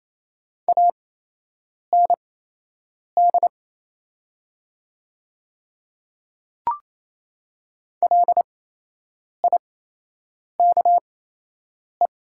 Morse code, heard as a keyed tone, about 28 wpm, spaced as for 8 wpm, 700 Hz. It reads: AND LIKE
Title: 8x Character Spacing - Top 300 Words - 28wpm